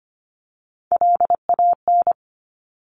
Morse code, heard as a keyed tone, about 25 wpm, 700 Hz. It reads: LAN
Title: Top Three Letter Combinations in Words 25wpm